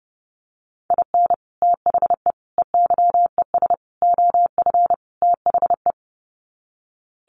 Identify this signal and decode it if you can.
Morse code, heard as a keyed tone, about 30 wpm, 700 Hz. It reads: IN THE EYES OF THE
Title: Sets of 5 Words 30wpm